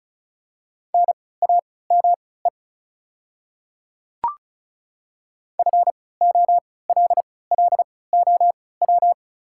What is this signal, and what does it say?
Morse code, heard as a keyed tone, about 36 wpm, spaced as for 20 wpm, 700 Hz. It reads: NAME FOLLOW
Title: 3x Character Spacing - Top 200 Words - 35wpm